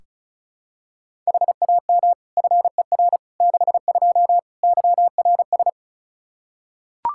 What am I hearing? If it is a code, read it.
Morse code, heard as a keyed tone, about 36 wpm, 700 Hz. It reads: HAM FER 62 YRS
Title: K1USN SST QSO Practice 35wpm